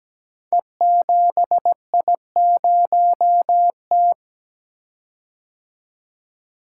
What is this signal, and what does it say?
Morse code, heard as a keyed tone, about 17 wpm, 700 Hz. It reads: E7I0T
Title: Code Groups - 5 Letter:Numbers - 17wpm